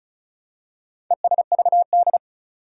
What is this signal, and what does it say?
Morse code, heard as a keyed tone, about 36 wpm, 700 Hz. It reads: ESVD
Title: Four Challenging Letter-Number Combinations 35wpm